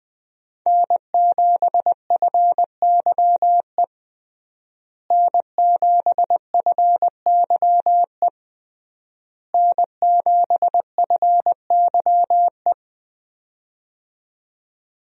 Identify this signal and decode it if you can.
Morse code, heard as a keyed tone, about 20 wpm, 700 Hz. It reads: N7FYE N7FYE N7FYE
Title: USA Call Signs - 5 Characters - Repeated 3x - 20wpm